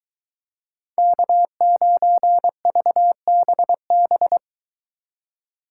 Morse code, heard as a keyed tone, about 23 wpm, 700 Hz. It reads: K9VBB